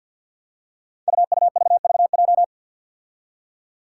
Morse code, heard as a keyed tone, about 46 wpm, 700 Hz. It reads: UUVVJ
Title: Five Challenging Letter-Number Combinations 50wpm